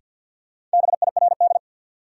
Morse code, heard as a keyed tone, about 50 wpm, 700 Hz. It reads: BIRD
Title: Sentences from Top 400 Words - New Word Drill - 50wpm